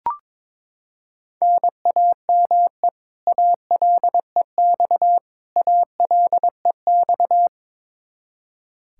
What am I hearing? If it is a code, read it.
Morse code, heard as a keyed tone, about 22 wpm, 700 Hz. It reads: NAME ALEX ALEX
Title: Practice QSO - Exchange 1 - Beginning - 22wpm